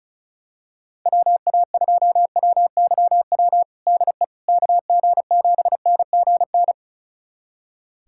Morse code, heard as a keyed tone, about 36 wpm, 700 Hz. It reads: WA2WYW DE KG7NGN
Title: USA Callsign DE Callsign - 6 Characters - 35wpm